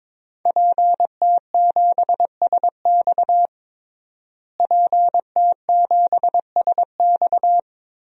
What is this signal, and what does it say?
Morse code, heard as a keyed tone, about 22 wpm, 700 Hz. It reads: PT7SX PT7SX
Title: DXCC-100 Call Signs - 5 Characters - Repeated 2x - 22wpm